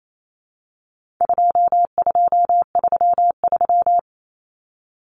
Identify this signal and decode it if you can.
Morse code, heard as a keyed tone, about 28 wpm, 700 Hz. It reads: 2233
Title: Common Cut Numbers - 4-Digit 28wpm